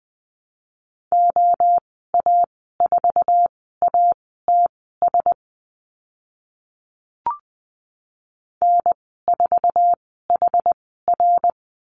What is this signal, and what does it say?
Morse code, heard as a keyed tone, about 20 wpm, spaced as for 14 wpm, 700 Hz. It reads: OA4ATS N4HR
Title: 062 Lesson 11g - Letter h - Call Signs 20wpm